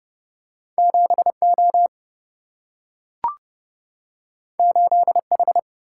Morse code, heard as a keyed tone, about 30 wpm, 700 Hz. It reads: 7 O 8 H
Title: Single-Letter Single-Number 30wpm